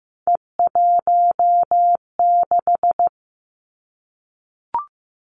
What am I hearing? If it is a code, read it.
Morse code, heard as a keyed tone, about 15 wpm, 700 Hz. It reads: E16
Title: Three Challenging Letter-Number Combinations 15wpm